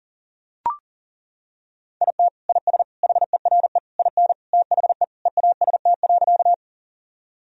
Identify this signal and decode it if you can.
Morse code, heard as a keyed tone, about 40 wpm, 700 Hz. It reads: IT IS HERE IN THE EAST.
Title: Sentences from Top 1000 Words - Review Part 3 - 40wpm